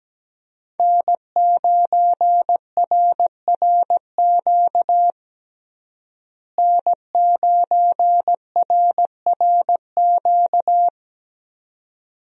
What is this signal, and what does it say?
Morse code, heard as a keyed tone, about 17 wpm, 700 Hz. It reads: N9RRQ N9RRQ
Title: USA Call Signs - 5 Characters - Repeated 2x - 17wpm